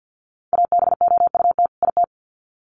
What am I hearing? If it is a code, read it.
Morse code, heard as a keyed tone, about 50 wpm, 700 Hz. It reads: ABOUT IT
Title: Sets of 2 Words 50wpm